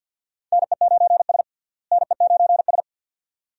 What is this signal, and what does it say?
Morse code, heard as a keyed tone, about 52 wpm, 700 Hz. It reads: NE9S NE9S
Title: USA Call Signs - 4 Characters - Repeated 2x - 50wpm